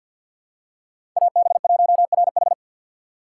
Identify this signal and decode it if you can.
Morse code, heard as a keyed tone, about 53 wpm, 700 Hz. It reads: AB1RH
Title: USA Call Signs - 5 Characters - 8x Character Spacing - 50wpm